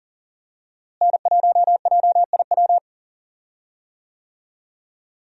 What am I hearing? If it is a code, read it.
Morse code, heard as a keyed tone, about 40 wpm, 700 Hz. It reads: N1JIW
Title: USA Call Signs - 5 Characters - Repeated 3x - 40wpm